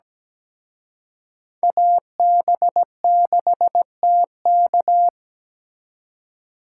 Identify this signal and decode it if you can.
Morse code, heard as a keyed tone, about 20 wpm, 700 Hz. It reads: AB6TK
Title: Speed Racing - USA 2x2 Call Signs 17wpm